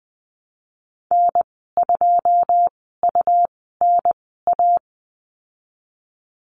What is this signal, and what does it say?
Morse code, heard as a keyed tone, about 20 wpm, spaced as for 14 wpm, 700 Hz. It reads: N2UNA
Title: 173 Lesson 28g - Number 9 - Call Signs 20wpm